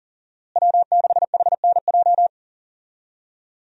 Morse code, heard as a keyed tone, about 40 wpm, 700 Hz. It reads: W6HNJ
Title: USA Call Signs - 5 Characters - Repeated 3x - 40wpm